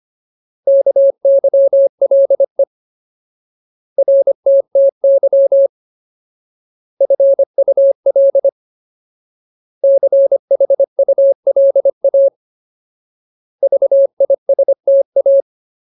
Morse code, heard as a keyed tone, about 25 wpm, 550 Hz. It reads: KYLE RTTY FUL CHULA VISTA
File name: Ultimate Word List / 25WPM Ultimate